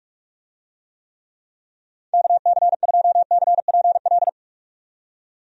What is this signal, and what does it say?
Morse code, heard as a keyed tone, about 41 wpm, 700 Hz. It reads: KC2CPL
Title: USA Call Signs - 6 Characters - 45wpm